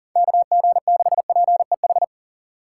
Morse code, heard as a keyed tone, about 52 wpm, 700 Hz. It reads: KG6PEH